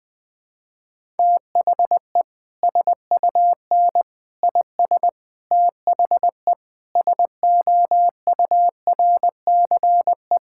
Morse code, heard as a keyed tone, about 21 wpm, 700 Hz. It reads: THE SUN IS THE SOURCE
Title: Sentences from Top 1000 Words - Review Part 5 - 20wpm